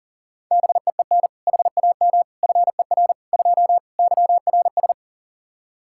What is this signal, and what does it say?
Morse code, heard as a keyed tone, about 40 wpm, 700 Hz. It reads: BEEN HAM FER 2 YRS
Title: K1USN SST QSO Practice 40wpm